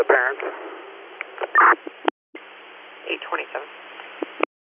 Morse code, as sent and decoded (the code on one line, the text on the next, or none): none